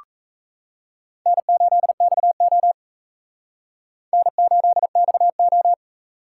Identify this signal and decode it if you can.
Morse code, heard as a keyed tone, about 38 wpm, 700 Hz. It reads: N8XO N8XO